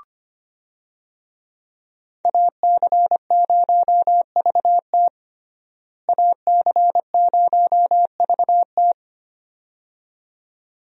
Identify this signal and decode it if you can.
Morse code, heard as a keyed tone, about 25 wpm, 700 Hz. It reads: AC0VT AC0VT
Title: USA Call Signs - 5 Characters - Repeated 2x - 25wpm